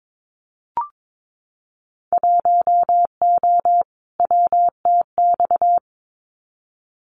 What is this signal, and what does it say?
Morse code, heard as a keyed tone, about 22 wpm, 700 Hz. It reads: 1O WTX
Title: Winter Field Day Exchanges Encore 22wpm